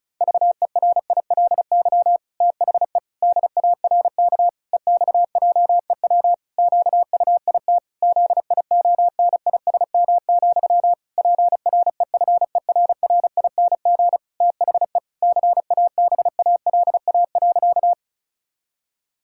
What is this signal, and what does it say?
Morse code, heard as a keyed tone, about 36 wpm, 700 Hz. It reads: VERILY THE DARK EXJEW QUIT ZIONISM, PREFERRING THE CABALA.
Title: Pangram Challenge v2 - 35wpm